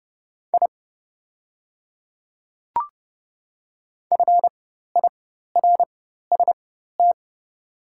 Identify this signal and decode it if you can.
Morse code, heard as a keyed tone, about 30 wpm, spaced as for 14 wpm, 690 Hz. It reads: I FIRST
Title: 4x Character Spacing - Top 100 Words - 30wpm